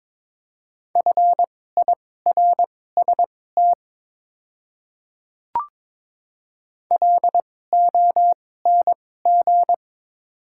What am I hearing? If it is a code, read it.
Morse code, heard as a keyed tone, about 22 wpm, spaced as for 16 wpm, 700 Hz. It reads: FIRST LONG
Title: 2x Character Spacing - Top 100 Words - 22wpm